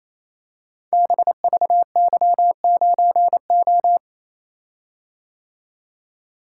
Morse code, heard as a keyed tone, about 28 wpm, 700 Hz. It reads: BVY9O
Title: Code Groups - 5 Letter:Numbers - 28wpm